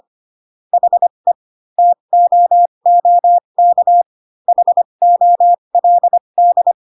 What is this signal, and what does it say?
Morse code, heard as a keyed tone, about 25 wpm, spaced as for 22 wpm, 700 Hz. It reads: HE TOOK HOLD
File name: Pinocchio / Pinocchio ch3 18 25 ik1hge